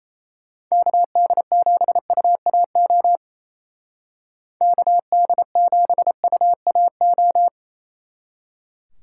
Morse code, heard as a keyed tone, about 34 wpm, 700 Hz. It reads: KD7UAO KD7UAO
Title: Speed Racing - USA 2x3 Call Signs 28wpm